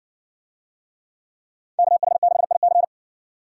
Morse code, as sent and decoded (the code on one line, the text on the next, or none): -.- .... -.... .. -.-.
KH6IC